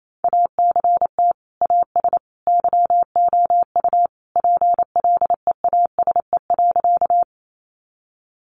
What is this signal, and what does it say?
Morse code, heard as a keyed tone, about 28 wpm, 700 Hz. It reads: ACT AS YOU PLEASE.
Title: Sentences from Top 1500 Words 28wpm